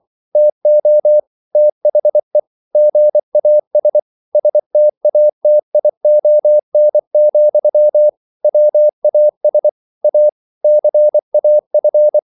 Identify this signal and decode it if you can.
Morse code, heard as a keyed tone, about 24 wpm, 600 Hz. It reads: TO THE GAS STATION, WAS A CAF